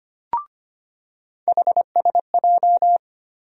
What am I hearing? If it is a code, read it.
Morse code, heard as a keyed tone, about 25 wpm, 700 Hz. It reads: HSJ
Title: USA Call Sign Postfixes - 3 Letters 25wpm